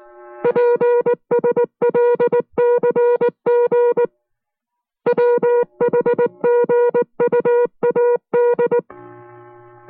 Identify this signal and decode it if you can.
Morse code, heard as a keyed tone, about 19 wpm, 480 Hz. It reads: PSLCG WHGUAD